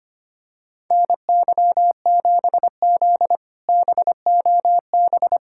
Morse code, heard as a keyed tone, about 25 wpm, 700 Hz. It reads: NY7Z BOB
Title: K1USN SST QSO Practice - 25wpm